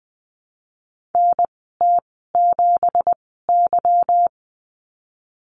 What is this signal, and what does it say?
Morse code, heard as a keyed tone, about 20 wpm, spaced as for 15 wpm, 700 Hz. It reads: NT7Y
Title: 245 Lesson 39g - Letter z - Call Signs 20wpm